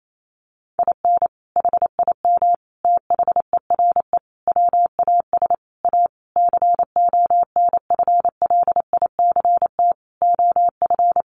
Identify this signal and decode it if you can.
Morse code, heard as a keyed tone, about 28 wpm, 700 Hz. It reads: IN HIM THERE WAS A CONFLICT OF
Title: Sentences from Top 1300 Words 28wpm